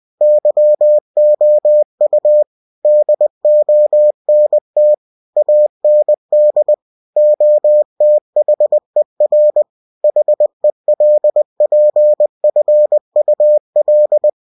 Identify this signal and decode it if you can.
Morse code, heard as a keyed tone, about 20 wpm, 600 Hz. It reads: YOU DONT AND OTHER HELPFUL
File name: cw-books / winnie-the-pooh / winnie-20wpm-page-0109